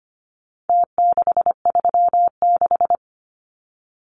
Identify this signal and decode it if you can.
Morse code, heard as a keyed tone, about 25 wpm, 700 Hz. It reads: T636
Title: Numbers - Contest Style - 4-Digit 25wpm